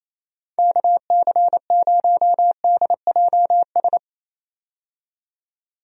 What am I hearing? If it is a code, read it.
Morse code, heard as a keyed tone, about 28 wpm, 700 Hz. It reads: KC0DJS
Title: USA Call Signs - 6 Characters - 28wpm